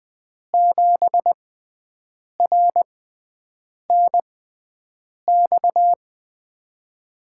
Morse code, heard as a keyed tone, about 20 wpm, spaced as for 7 wpm, 700 Hz. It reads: 7RNX
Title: USA Call Signs - 6x Character Spacing - 20wpm